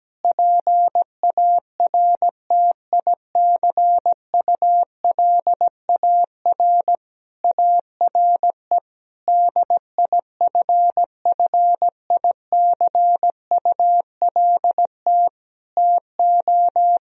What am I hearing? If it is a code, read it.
Morse code, heard as a keyed tone, about 17 wpm, 700 Hz. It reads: PARTICULAR ARE DIFFICULT TO
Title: Sentences from Top 1000 Words - Review Part 4 - 17wpm